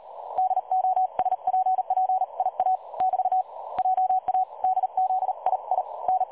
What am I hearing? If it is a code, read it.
Morse code, heard as a keyed tone, about 38 wpm, 750 Hz. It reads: DOSPP IA <BT> JA DZI I N
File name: dlazaawansowanych / nagraniazpasma / websdr recording 3565.0kHz